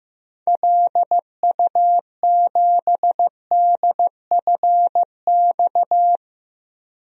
Morse code, heard as a keyed tone, about 15 wpm, 700 Hz. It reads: LU7DFX
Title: DXCC-100 Call Signs - 6 Characters - Repeated 2x - 15wpm